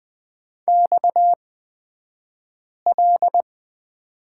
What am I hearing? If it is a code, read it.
Morse code, heard as a keyed tone, about 20 wpm, spaced as for 5 wpm, 700 Hz. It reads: XL
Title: 228 Lesson 37d - Letter x - Two Characters 20wpm